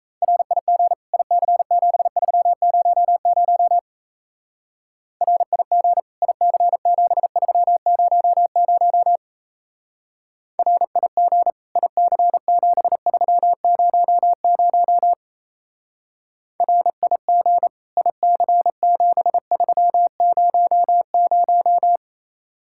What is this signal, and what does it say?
Morse code, heard as a keyed tone, about 33 wpm, 700 Hz. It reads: RIG IC7300 RIG IC7300 RIG IC7300 RIG IC7300
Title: Speed-Racing Common QSO Phrases - 28wpm